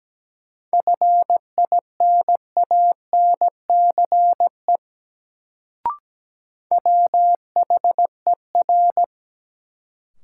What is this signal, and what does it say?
Morse code, heard as a keyed tone, about 17 wpm, 700 Hz. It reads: FINANCE WHER
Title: Sentences from Top 1500 Words - New Word Drill Alt - 17wpm